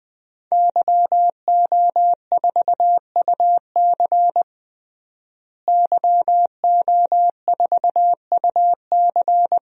Morse code, heard as a keyed tone, about 21 wpm, 700 Hz. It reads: YO4UC YO4UC
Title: DXCC-100 Call Signs - 5 Characters - Repeated 2x - 20wpm